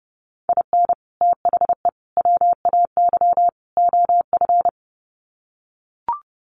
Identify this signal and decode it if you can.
Morse code, heard as a keyed tone, about 30 wpm, 700 Hz. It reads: IN THE WAY OF
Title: Sets of 4 Words 30wpm